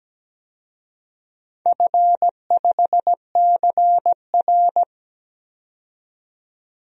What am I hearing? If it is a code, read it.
Morse code, heard as a keyed tone, about 17 wpm, 700 Hz. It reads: F5CR
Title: DXCC-100 Call Signs - 4 Characters - 17wpm